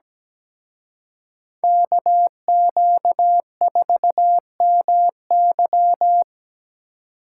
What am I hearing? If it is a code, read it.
Morse code, heard as a keyed tone, about 20 wpm, 700 Hz. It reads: KQ4MY
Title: Speed Racing - USA 2x2 Call Signs 17wpm